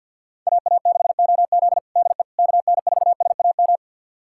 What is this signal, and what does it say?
Morse code, heard as a keyed tone, about 47 wpm, 700 Hz. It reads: AA6OZ DE KN4SAM
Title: USA Callsign DE Callsign 50wpm